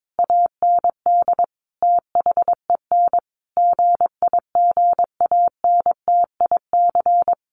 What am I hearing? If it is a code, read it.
Morse code, heard as a keyed tone, about 22 wpm, 700 Hz. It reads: AND THEN GIGANTIC
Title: A Sound Of Thunder 22wpm